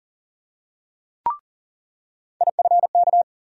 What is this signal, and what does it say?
Morse code, heard as a keyed tone, about 40 wpm, 700 Hz. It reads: IFK
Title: USA Call Sign Postfixes - 3 Letters 40wpm